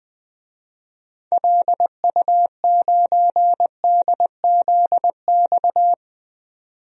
Morse code, heard as a keyed tone, about 21 wpm, 700 Hz. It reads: LU9DZX
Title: DXCC-100 Call Signs - 6 Characters - 20wpm